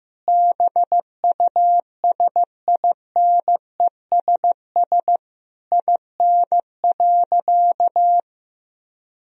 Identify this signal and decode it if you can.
Morse code, heard as a keyed tone, about 16 wpm, 700 Hz. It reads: BUSINESS IN.
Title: Sentences from Top 1000 Words - Review Part 3 - 15wpm